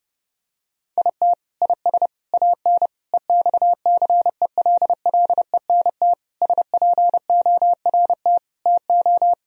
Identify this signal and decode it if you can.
Morse code, heard as a keyed tone, about 30 wpm, 700 Hz. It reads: IT IS AN EXCELLENT SPORT TO
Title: Sentences from Top 1000 Words - Review Part 4 - 30wpm